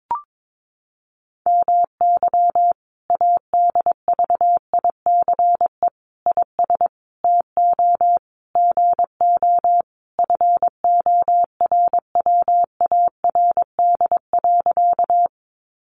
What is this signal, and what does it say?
Morse code, heard as a keyed tone, about 22 wpm, 700 Hz. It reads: MY ADVICE IS TO GO FORWARD.
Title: Sentences from Top 1300 Words 22wpm